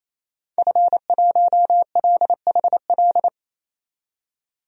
Morse code, heard as a keyed tone, about 28 wpm, 700 Hz. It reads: F1LHL
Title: DXCC-100 Call Signs - 5 Characters - 28wpm